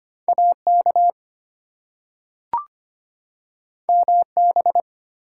Morse code, heard as a keyed tone, about 25 wpm, 700 Hz. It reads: AK MB
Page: morse-code-ninja-practice-25wpm